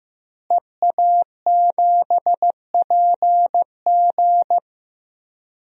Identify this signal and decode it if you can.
Morse code, heard as a keyed tone, about 15 wpm, 700 Hz. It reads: EA7PG
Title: DXCC-100 Call Signs - 5 Characters - Repeated 3x - 15wpm